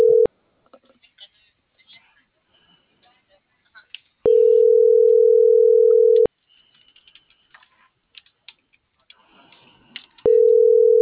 Ringback tone (North America), 440 + 480 Hz.